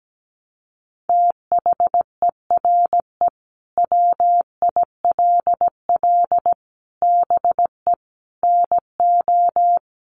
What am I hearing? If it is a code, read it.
Morse code, heard as a keyed tone, about 17 wpm, 700 Hz. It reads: THERE WILL BE NO
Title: Sentences from Top 1300 Words 17wpm